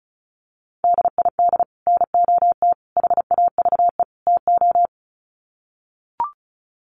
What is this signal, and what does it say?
Morse code, heard as a keyed tone, about 34 wpm, 700 Hz. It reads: DID NOT HAVE TO
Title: Sets of 4 Words 35wpm